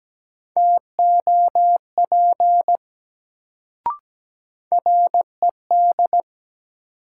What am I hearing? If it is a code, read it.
Morse code, heard as a keyed tone, about 17 wpm, 700 Hz. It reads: TOP RED
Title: Sentences from Top 700 Words - New Word Drill - 17wpm